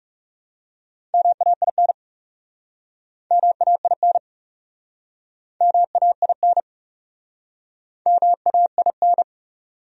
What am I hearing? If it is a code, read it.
Morse code, heard as a keyed tone, about 40 wpm, 700 Hz. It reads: MAIN MAIN MAIN MAIN